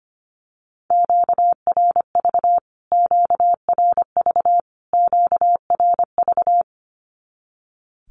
Morse code, heard as a keyed tone, about 25 wpm, 700 Hz. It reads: QRV QRV QRV
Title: Q-Codes Repeated 3x 25wpm